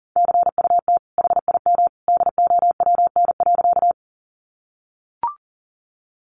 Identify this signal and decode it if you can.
Morse code, heard as a keyed tone, about 40 wpm, 700 Hz. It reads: CUT HIM DOWN.
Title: Sentences from Top 300 Words 40wpm